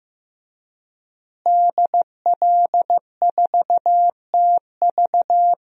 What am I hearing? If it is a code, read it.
Morse code, heard as a keyed tone, about 15 wpm, 700 Hz. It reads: DL4TV
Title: DXCC-100 Call Signs - 5 Characters - Repeated 3x - 15wpm